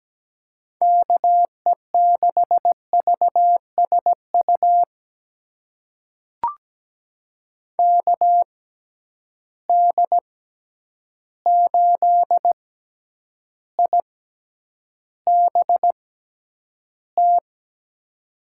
Morse code, heard as a keyed tone, about 17 wpm, spaced as for 9 wpm, 700 Hz. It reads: KE6VSU KD8IBT
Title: USA Call Signs - 6 Characters - 6x Character Spacing - 17wpm